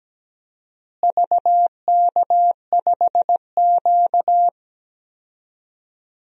Morse code, heard as a keyed tone, about 17 wpm, 700 Hz. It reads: VK5Q
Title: DXCC-100 Call Signs - 4 Characters - Repeated 2x - 17wpm